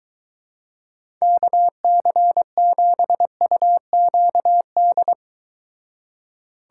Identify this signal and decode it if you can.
Morse code, heard as a keyed tone, about 23 wpm, 700 Hz. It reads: KC7UQD